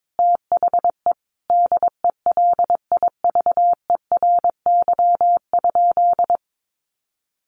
Morse code, heard as a keyed tone, about 22 wpm, 700 Hz. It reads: THE DELIVERY?
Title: Sentences from Top 1500 Words 22wpm